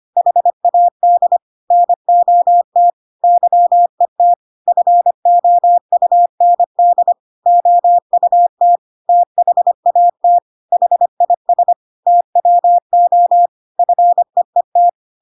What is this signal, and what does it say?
Morse code, heard as a keyed tone, about 25 wpm, 700 Hz. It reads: HAD NOT YET FOUND OUT THAT HIS TWO FEET
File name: Pinocchio / Pinocchio ch7 25 25 ik1hge